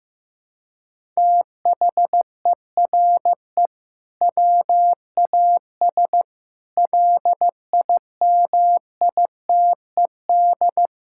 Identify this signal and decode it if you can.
Morse code, heard as a keyed tone, about 15 wpm, 700 Hz. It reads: THERE WAS LIMITED
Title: Sentences from Top 1000 Words - Review Part 6 - 15wpm